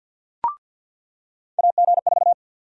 Morse code, heard as a keyed tone, about 47 wpm, 700 Hz. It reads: AG4
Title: USA Call Sign Prefixes - 50wpm